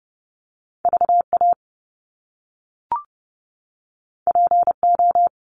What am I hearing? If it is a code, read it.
Morse code, heard as a keyed tone, about 30 wpm, 700 Hz. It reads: VA PO
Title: Top Two Letter Combinations in Words 30wpm